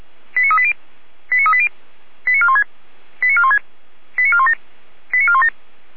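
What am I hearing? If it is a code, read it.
Five-tone selective call (ZVEI), heard as quick runs of five tones, tones of about 70 ms. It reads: 89380 89380 89416 89416 89417 89417
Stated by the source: ZVEI five-tone call received by SDR.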